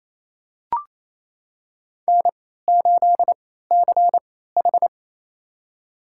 Morse code, heard as a keyed tone, about 28 wpm, spaced as for 16 wpm, 700 Hz. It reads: N8CH
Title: USA Call Signs - 4 Characters - 3x Character Spacing - 28wpm